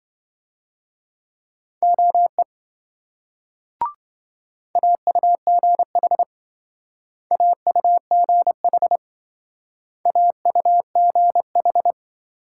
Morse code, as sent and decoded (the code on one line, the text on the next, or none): --- . / .- ..- --. .... / .- ..- --. .... / .- ..- --. ....
OE AUGH AUGH AUGH